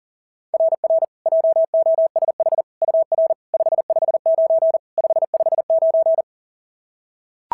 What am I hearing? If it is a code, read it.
Morse code, heard as a keyed tone, about 37 wpm, 690 Hz. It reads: RR JOSH UR 559 559
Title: SOTA QSO Practice 40wpm